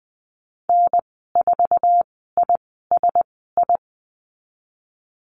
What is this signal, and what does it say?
Morse code, heard as a keyed tone, about 20 wpm, spaced as for 14 wpm, 700 Hz. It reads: N4ISI
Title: 055 Lesson 10g - Letter r - Call Signs 20wpm